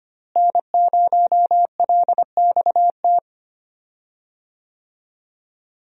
Morse code, heard as a keyed tone, about 25 wpm, 700 Hz. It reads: N0LXT
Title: USA Call Signs - 5 Characters - Repeated 3x - 25wpm